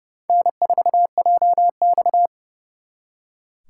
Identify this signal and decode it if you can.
Morse code, heard as a keyed tone, about 38 wpm, 700 Hz. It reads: N4JX
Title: Speed Racing - USA 1x2 Call Signs 30wpm